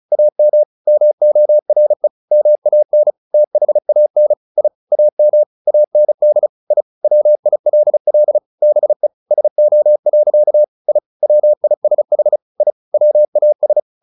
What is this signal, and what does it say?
Morse code, heard as a keyed tone, about 34 wpm, 600 Hz. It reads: AM MORE MAN THAN I AM AND I WILL BE SO. I WISH I WAS
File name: the old man and the sea / the old man and the sea 1 35wpm